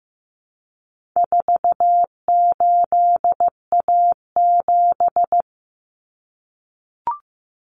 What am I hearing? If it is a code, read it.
Morse code, heard as a keyed tone, about 15 wpm, 700 Hz. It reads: 48A7
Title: Common Cut Numbers - 4-Digit 15wpm